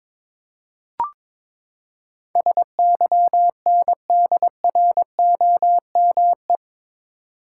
Morse code, heard as a keyed tone, about 22 wpm, 700 Hz. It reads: SYNDROME
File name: practice / 22wpm / Eight Letter Words - 22wpm